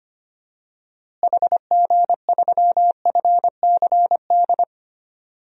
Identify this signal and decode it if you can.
Morse code, heard as a keyed tone, about 25 wpm, 700 Hz. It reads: HG3FCD
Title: DXCC-100 Call Signs - 6 Characters - Repeated 2x - 25wpm